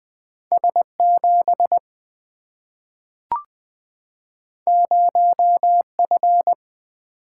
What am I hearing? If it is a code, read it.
Morse code, heard as a keyed tone, about 20 wpm, 700 Hz. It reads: S7 0F
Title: Single-Letter Single-Number 20wpm